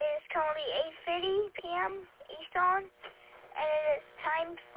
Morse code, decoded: none

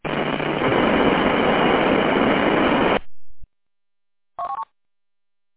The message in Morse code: none

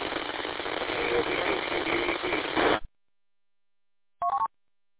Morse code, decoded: none

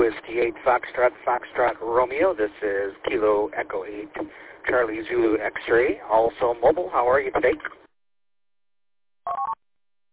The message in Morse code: none

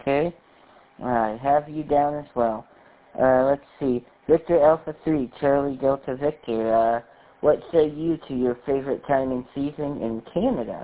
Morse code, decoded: none

